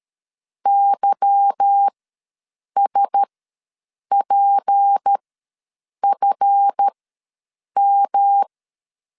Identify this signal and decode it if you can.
Morse code, heard as a keyed tone, about 13 wpm, spaced as for 7 wpm, 800 Hz. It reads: YSPFM
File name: LEZIONI / 8 YJ45 / 8 Y ws 12wpm 3409